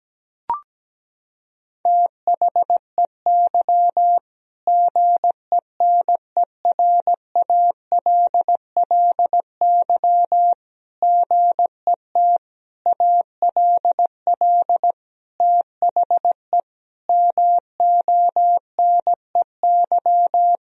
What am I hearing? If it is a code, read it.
Morse code, heard as a keyed tone, about 17 wpm, 700 Hz. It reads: THEY GENERALLY GET ALL THE MONEY
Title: Sentences from Top 1000 Words - Review Part 4 - 17wpm